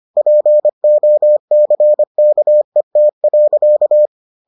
Morse code, heard as a keyed tone, about 25 wpm, 600 Hz. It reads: POCKET.